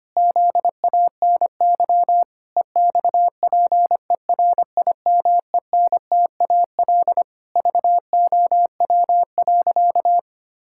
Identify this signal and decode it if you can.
Morse code, heard as a keyed tone, about 25 wpm, 700 Hz. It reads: ZANY EXPERIMENTAL VOW.
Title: Pangram Challenge v2 - 25wpm